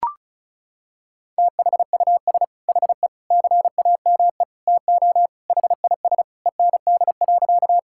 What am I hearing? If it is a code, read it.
Morse code, heard as a keyed tone, about 36 wpm, 700 Hz. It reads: THUS HE CAME TO HIS END.
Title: Sentences from Top 1000 Words - Review Part 5 - 35wpm